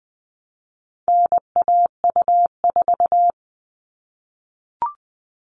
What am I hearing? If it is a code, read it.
Morse code, heard as a keyed tone, about 20 wpm, 700 Hz. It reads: NAU4
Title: All Cut Numbers - 4-Digit 20wpm